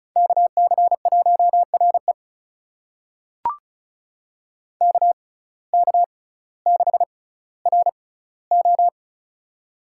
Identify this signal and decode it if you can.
Morse code, heard as a keyed tone, about 36 wpm, spaced as for 12 wpm, 700 Hz. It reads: KC1REK K6RO
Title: USA Call Signs - 6x Character Spacing - 35wpm